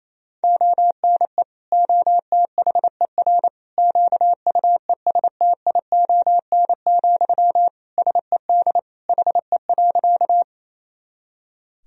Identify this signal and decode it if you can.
Morse code, heard as a keyed tone, about 27 wpm, 700 Hz. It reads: ONE OTHER QUESTION, SED HE.
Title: Sentences from Top 400 Words and Abbreviations 28wpm